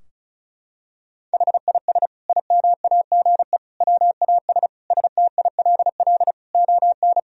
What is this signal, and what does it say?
Morse code, heard as a keyed tone, about 36 wpm, 700 Hz. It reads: HIS IMAGE WAS STILL ON
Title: Sentences from Top 1000 Words - Review Part 4 - 35wpm